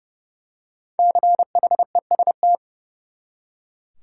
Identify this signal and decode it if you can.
Morse code, heard as a keyed tone, about 30 wpm, 700 Hz. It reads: CHEST
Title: Five Letter Words - 30wpm